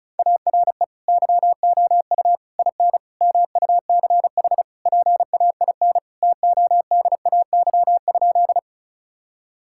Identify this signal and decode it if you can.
Morse code, heard as a keyed tone, about 36 wpm, 700 Hz. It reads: ARE YOU IN MUCH PAIN TODAY?
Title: Sentences from Top 1600 Words 35wpm